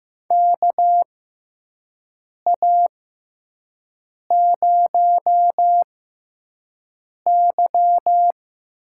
Morse code, heard as a keyed tone, about 15 wpm, spaced as for 5 wpm, 700 Hz. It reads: KA0Y
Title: USA Call Signs - 6 Characters - 6x Character Spacing - 15wpm